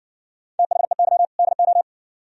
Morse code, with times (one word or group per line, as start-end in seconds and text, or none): 0.6-1.3 THEY
1.4-1.8 DO